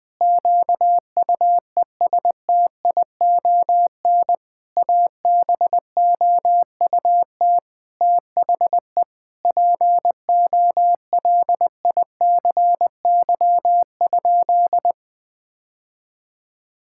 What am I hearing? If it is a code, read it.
Morse code, heard as a keyed tone, about 20 wpm, 700 Hz. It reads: QUESTION ABOUT THE POLICY?
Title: Sentences with 6-Word Phrases - 20wpm